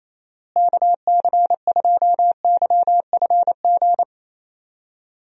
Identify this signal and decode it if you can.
Morse code, heard as a keyed tone, about 28 wpm, 700 Hz. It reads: KC2YFG
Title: USA Call Signs - 6 Characters - Repeated 2x - 28wpm